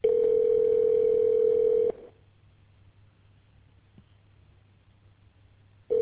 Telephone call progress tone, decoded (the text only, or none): ringback tone